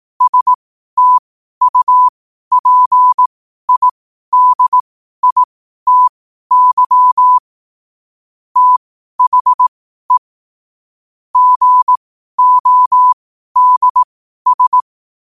Morse code, heard as a keyed tone, about 18 wpm, spaced as for 12 wpm, 1000 Hz. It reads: STUPIDITY THE GODS
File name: qotdfiles / 12wpm